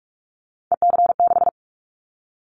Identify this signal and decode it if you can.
Morse code, heard as a keyed tone, about 46 wpm, 700 Hz. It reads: EC6
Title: Callsign Prefixes for Top 50 DXCC - 45wpm